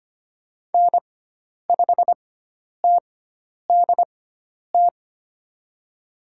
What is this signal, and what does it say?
Morse code, heard as a keyed tone, about 25 wpm, spaced as for 10 wpm, 700 Hz. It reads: N5TDT
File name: Lesson 16g - Letter c - Call Signs 25wpm